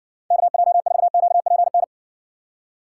Morse code, heard as a keyed tone, about 61 wpm, 700 Hz. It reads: KJ4QPN